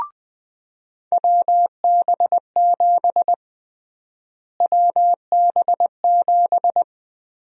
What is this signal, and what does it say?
Morse code, heard as a keyed tone, about 20 wpm, 700 Hz. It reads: WB7 WB7
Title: USA Call Sign Prefixes - Repeated 2x - 20wpm